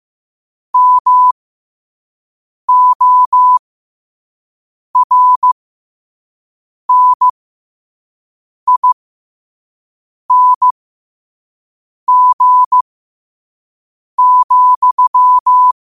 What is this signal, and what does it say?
Morse code, heard as a keyed tone, about 15 wpm, spaced as for 5 wpm, 1000 Hz. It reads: MORNING,
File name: qotdfiles / 5wpm